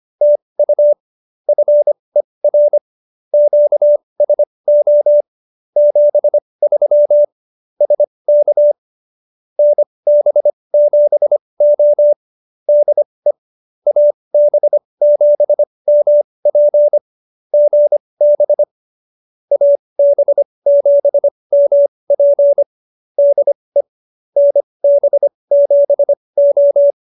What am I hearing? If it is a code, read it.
Morse code, heard as a keyed tone, about 25 wpm, spaced as for 20 wpm, 590 Hz. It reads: TU FER QSO 73 SK NB7O DE AB7MP GB AB7MP DE NB7O